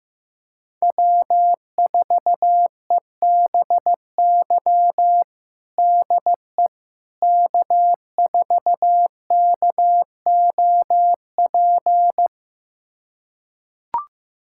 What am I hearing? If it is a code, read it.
Morse code, heard as a keyed tone, about 15 wpm, 700 Hz. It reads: W4EBY DE K4KOP
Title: USA Callsign DE Callsign - 5 Characters - 15wpm